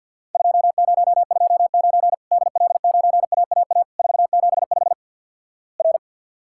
Morse code, heard as a keyed tone, about 50 wpm, 700 Hz. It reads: 2029 DL9AAA 475 R
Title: Worked All Europe DX Contest - QTC Exchange with Repeats - 50wpm